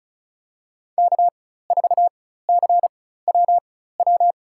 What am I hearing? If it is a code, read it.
Morse code, heard as a keyed tone, about 34 wpm, spaced as for 16 wpm, 700 Hz. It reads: K4CWW
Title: USA Call Signs - 5 Characters - 4x Character Spacing - 35wpm